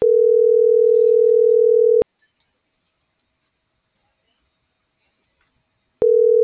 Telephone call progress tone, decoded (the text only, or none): ringback tone